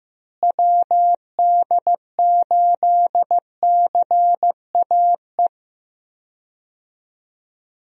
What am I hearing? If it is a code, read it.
Morse code, heard as a keyed tone, about 15 wpm, 700 Hz. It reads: WD8CAE
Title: USA Call Signs - 6 Characters - Repeated 3x - 15wpm